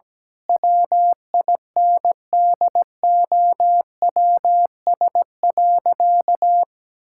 Morse code, heard as a keyed tone, about 17 wpm, 700 Hz. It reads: WINDOWS.
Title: Pangram Challenge v2 - 17wpm